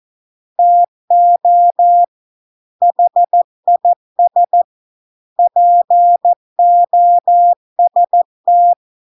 Morse code, heard as a keyed tone, about 14 wpm, 700 Hz. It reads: TO HIS POST